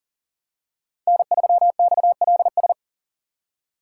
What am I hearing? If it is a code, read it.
Morse code, heard as a keyed tone, about 40 wpm, 700 Hz. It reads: N3XLS
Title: K1USN SST QSO Practice - 40wpm